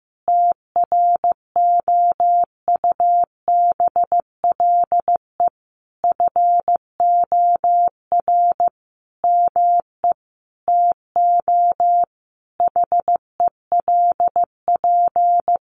Morse code, heard as a keyed tone, about 15 wpm, 700 Hz. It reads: TROUBLE FOR ME TO HELP
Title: Sentences from Top 1200 Words 15wpm